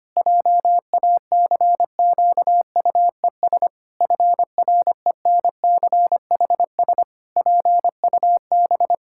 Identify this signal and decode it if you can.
Morse code, heard as a keyed tone, about 25 wpm, 700 Hz. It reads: JACQUES FRENCHS PUB
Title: Pangram Challenge v2 - 25wpm